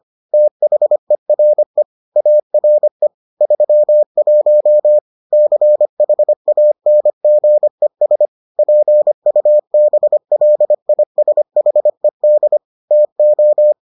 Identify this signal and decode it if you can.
Morse code, heard as a keyed tone, about 25 wpm, 600 Hz. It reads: THERE ARE 31 CHANGES PUBLISHED TO